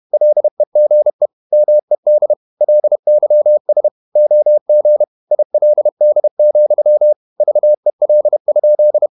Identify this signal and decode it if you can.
Morse code, heard as a keyed tone, about 31 wpm, 600 Hz. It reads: LEGE MED LYS OG ILD, VEL?
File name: Ferie QRQ 31wpm